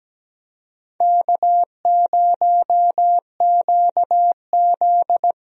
Morse code, heard as a keyed tone, about 17 wpm, 700 Hz. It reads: K0QZ
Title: USA Call Signs - 4 Characters - Repeated 3x - 17wpm